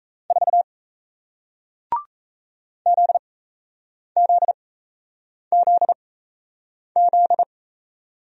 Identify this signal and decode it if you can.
Morse code, heard as a keyed tone, about 38 wpm, spaced as for 8 wpm, 700 Hz. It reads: 4 ZZZZ